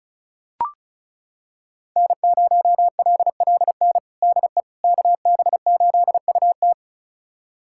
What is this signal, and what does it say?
Morse code, heard as a keyed tone, about 36 wpm, 700 Hz. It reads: N0LLN DE KB8UT
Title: USA Callsign DE Callsign - 5 Characters - 35wpm